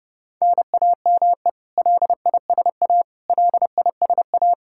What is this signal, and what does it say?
Morse code, heard as a keyed tone, about 32 wpm, 700 Hz. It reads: NAME LISA LISA
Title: Practice QSO - Exchange 1 - Beginning - 30wpm